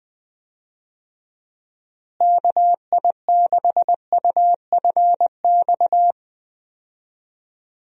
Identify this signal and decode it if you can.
Morse code, heard as a keyed tone, about 20 wpm, 700 Hz. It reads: KI6UFX
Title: USA Call Signs - 6 Characters - Repeated 2x - 20wpm